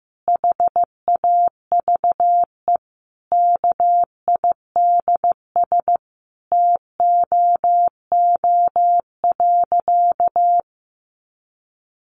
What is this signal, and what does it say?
Morse code, heard as a keyed tone, about 15 wpm, 700 Hz. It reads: HAVE KIDS TOO.
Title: Easy Phrases 15wpm